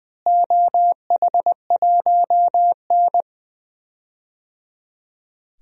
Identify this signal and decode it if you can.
Morse code, heard as a keyed tone, about 20 wpm, 700 Hz. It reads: OH1N
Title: DXCC-100 Call Signs - 4 Characters - Repeated 3x - 20wpm